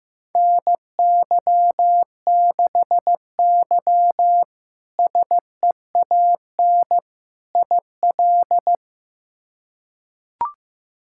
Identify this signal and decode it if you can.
Morse code, heard as a keyed tone, about 16 wpm, 700 Hz. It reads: NY6Y SEAN IL
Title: K1USN SST QSO Practice - 15wpm